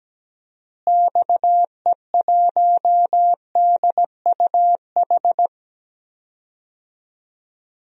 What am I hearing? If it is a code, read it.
Morse code, heard as a keyed tone, about 17 wpm, 700 Hz. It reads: XE1DUH
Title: DXCC-100 Call Signs - 6 Characters - Repeated 2x - 17wpm